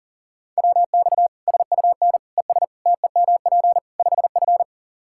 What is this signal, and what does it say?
Morse code, heard as a keyed tone, about 40 wpm, 700 Hz. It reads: WX SUN ES TEMP 5F
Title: Practice QSO - Exchange 2 - Info - 40wpm